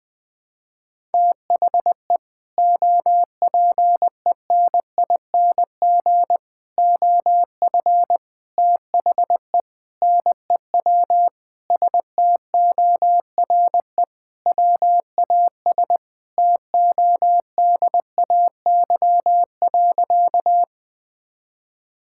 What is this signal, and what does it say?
Morse code, heard as a keyed tone, about 20 wpm, 700 Hz. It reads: THE OPENING OF THE NEW STORE WAS TODAY.
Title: Sentences with 3-Word Phrases - 20wpm